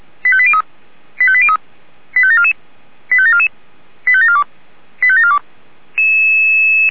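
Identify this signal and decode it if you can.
Five-tone selective call (ZVEI), heard as quick runs of five tones, tones of about 70 ms. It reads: 85803 85803 85744 85744 85742 85742
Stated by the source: ZVEI five-tone call 85730 received on an SDR.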